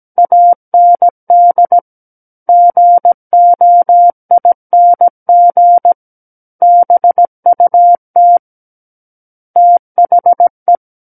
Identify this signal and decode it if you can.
Morse code, heard as a keyed tone, about 17 wpm, 700 Hz. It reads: AND GOING BUT THE
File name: Book One / 08-Chapter 8 - 17 WPM